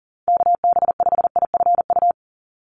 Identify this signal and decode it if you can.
Morse code, heard as a keyed tone, about 38 wpm, 700 Hz. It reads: KB5IFU
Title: USA Call Signs 40wpm